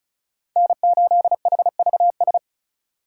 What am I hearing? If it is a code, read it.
Morse code, heard as a keyed tone, about 36 wpm, 700 Hz. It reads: N8HVS
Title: USA Call Signs - 4x Character Spacing - 35wpm